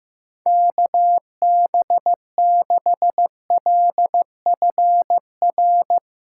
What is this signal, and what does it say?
Morse code, heard as a keyed tone, about 16 wpm, 700 Hz. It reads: KB6LFR
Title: USA Callsign DE Callsign 15wpm